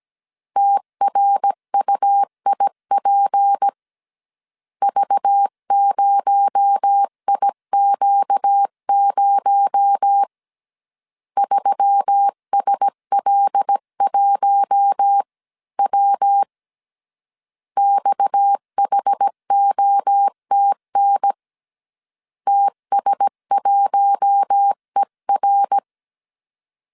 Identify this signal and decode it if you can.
Morse code, heard as a keyed tone, about 17 wpm, 800 Hz. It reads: TRUIP V0IQ0 3SL1W XHOTN TS1ER
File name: CW / DET / 1 6 1500C S C 16wpm 2733